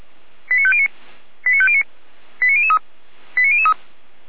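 Five-tone selective call (ZVEI), heard as quick runs of five tones, tones of about 70 ms. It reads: 89509 89509 89003 89003
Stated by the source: ZVEI five-tone call received by SDR, from audio.f85360.feuerwehr-orlamuende.org (Feuerwehr Orlamünde).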